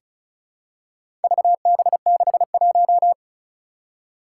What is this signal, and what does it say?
Morse code, heard as a keyed tone, about 36 wpm, 700 Hz. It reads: VB61
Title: Four Challenging Letter-Number Combinations 35wpm